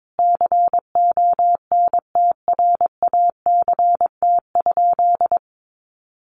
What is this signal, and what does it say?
Morse code, heard as a keyed tone, about 22 wpm, 700 Hz. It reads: CONTRACT?
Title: Sentences from Top 1400 Words 22wpm